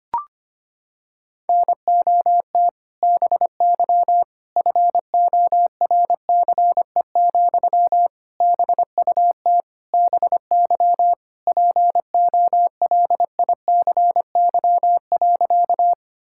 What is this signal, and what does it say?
Morse code, heard as a keyed tone, about 25 wpm, 700 Hz. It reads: NOT BY FORCE, BUT BY POLICY.
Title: Sentences from Top 1000 Words - Review Part 5 - 25wpm